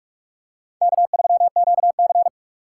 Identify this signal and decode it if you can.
Morse code, heard as a keyed tone, about 44 wpm, 700 Hz. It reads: K3QC
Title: USA Call Signs - 2x Character Spacing - 45wpm